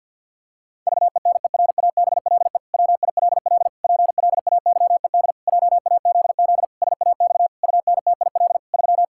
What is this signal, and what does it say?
Morse code, heard as a keyed tone, about 52 wpm, 700 Hz. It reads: VENERABLE WILL PLAYED JAZZ SAX UNTIL 3
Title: Pangram Challenge v2 - 50wpm